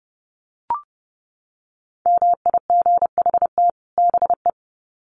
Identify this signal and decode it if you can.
Morse code, heard as a keyed tone, about 30 wpm, 700 Hz. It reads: MIGHT BE
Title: Sets of 2 Words 30wpm